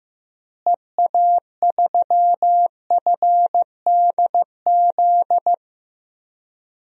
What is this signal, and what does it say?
Morse code, heard as a keyed tone, about 15 wpm, 700 Hz. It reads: EA3FDZ
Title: DXCC-100 Call Signs - 6 Characters - Repeated 3x - 15wpm